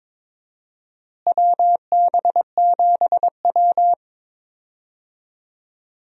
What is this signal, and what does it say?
Morse code, heard as a keyed tone, about 22 wpm, 700 Hz. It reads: WB7W
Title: USA Call Signs - 4 Characters - Repeated 3x - 22wpm